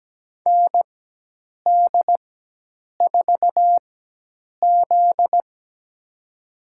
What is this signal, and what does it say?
Morse code, heard as a keyed tone, about 17 wpm, spaced as for 8 wpm, 700 Hz. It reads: ND4Z
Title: USA Call Signs - 4x Character Spacing - 17wpm